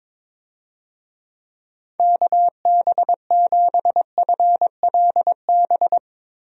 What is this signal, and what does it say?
Morse code, heard as a keyed tone, about 22 wpm, 700 Hz. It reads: KB7FLB
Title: USA Call Signs - 6 Characters - 22wpm